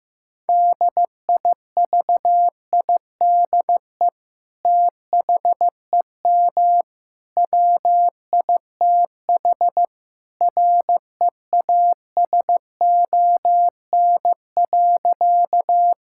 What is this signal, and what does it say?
Morse code, heard as a keyed tone, about 15 wpm, 700 Hz. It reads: DIVIDE THEM WITH REASON.
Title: Sentences from Top 1600 Words 15wpm